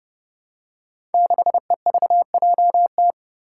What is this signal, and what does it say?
Morse code, heard as a keyed tone, about 29 wpm, 700 Hz. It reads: 6EVJT
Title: Five Challenging Letter-Number Combinations 30wpm